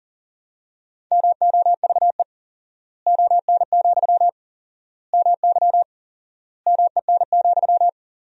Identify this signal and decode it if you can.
Morse code, heard as a keyed tone, about 40 wpm, 700 Hz. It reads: MOVE ON, MY MEN,
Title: 4x Word-Spacing - Sentences from Top 500 Words - 40wpm